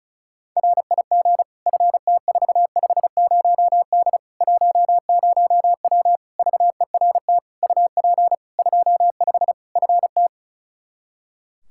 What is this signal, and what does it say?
Morse code, heard as a keyed tone, about 34 wpm, 700 Hz. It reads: RIG FT450D 10W VERT UP 25 FT
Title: Practice QSO - Exchange 2 - Info - 35wpm